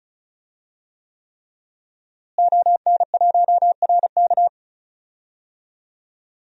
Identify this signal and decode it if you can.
Morse code, heard as a keyed tone, about 36 wpm, 700 Hz. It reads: ON1RK
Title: DXCC-100 Call Signs - 5 Characters - Repeated 3x - 35wpm